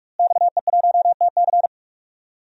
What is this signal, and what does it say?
Morse code, heard as a keyed tone, about 48 wpm, 700 Hz. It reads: XE1TC
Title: DXCC-100 Call Signs - 5 Characters - Repeated 3x - 45wpm